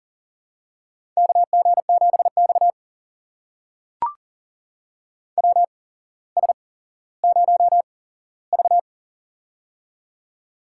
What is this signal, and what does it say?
Morse code, heard as a keyed tone, about 40 wpm, spaced as for 17 wpm, 700 Hz. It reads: KG7X WS0V
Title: USA Call Signs - 4 Characters - 8x Character Spacing - 40wpm